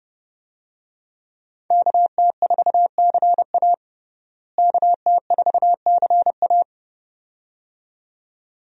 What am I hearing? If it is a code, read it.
Morse code, heard as a keyed tone, about 30 wpm, 700 Hz. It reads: KT4CA KT4CA
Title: USA Call Signs - 5 Characters - Repeated 2x - 30wpm